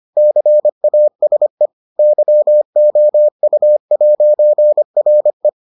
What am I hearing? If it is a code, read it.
Morse code, heard as a keyed tone, about 25 wpm, 600 Hz. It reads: CASE YOU'RE